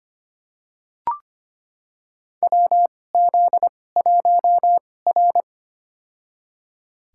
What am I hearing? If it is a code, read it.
Morse code, heard as a keyed tone, about 25 wpm, spaced as for 18 wpm, 700 Hz. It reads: WZ1R
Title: USA Call Signs - 4 Characters - 2x Character Spacing - 25wpm